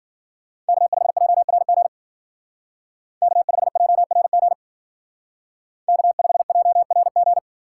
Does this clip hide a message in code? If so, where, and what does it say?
Yes: Morse code from 0.7 to 7.4 s, heard as a keyed tone, about 52 wpm, 700 Hz. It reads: K5JRG K5JRG K5JRG